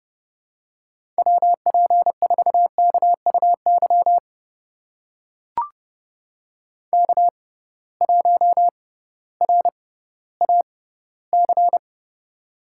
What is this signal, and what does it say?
Morse code, heard as a keyed tone, about 32 wpm, spaced as for 17 wpm, 700 Hz. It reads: WP4KUY K1RAC
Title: USA Call Signs - 6x Character Spacing - 30wpm